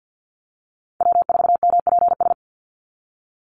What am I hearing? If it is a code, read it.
Morse code, heard as a keyed tone, about 50 wpm, 700 Hz. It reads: W4MPS
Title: NA SOTA Chaser Callsigns 50wpm